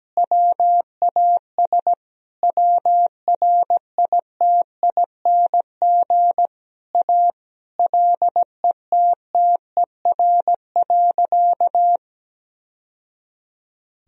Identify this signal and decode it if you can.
Morse code, heard as a keyed tone, about 17 wpm, 700 Hz. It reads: WAS WRITING A LETTER.
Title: Sentences from Top 1000 Words - Review Part 4 - 17wpm